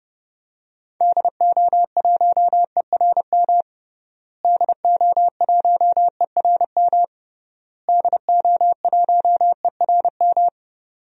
Morse code, heard as a keyed tone, about 32 wpm, 700 Hz. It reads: DO1ERM DO1ERM DO1ERM
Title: DXCC-100 Call Signs - 6 Characters - Repeated 3x - 30wpm